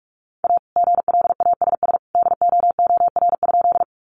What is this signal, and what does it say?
Morse code, heard as a keyed tone, about 44 wpm, 700 Hz. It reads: A GLASS DOOR?
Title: Sentences from Top 1300 Words 45wpm